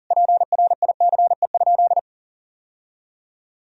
Morse code, heard as a keyed tone, about 40 wpm, 700 Hz. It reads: PRICE?
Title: Sentences from Top 1000 Words - Review Part 3 - 40wpm